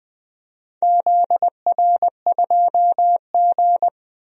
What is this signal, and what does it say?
Morse code, heard as a keyed tone, about 20 wpm, 700 Hz. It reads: ZR2G
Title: DXCC-100 Call Signs - 4 Characters - Repeated 3x - 20wpm